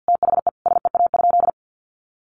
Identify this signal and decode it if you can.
Morse code, heard as a keyed tone, about 50 wpm, 700 Hz. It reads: THE SEA?
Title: Sentences from Top 300 Words 50wpm